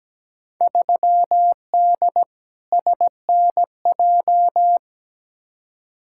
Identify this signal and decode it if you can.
Morse code, heard as a keyed tone, about 17 wpm, 700 Hz. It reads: 3D SNJ
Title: Field Day Exchanges Encore - 17wpm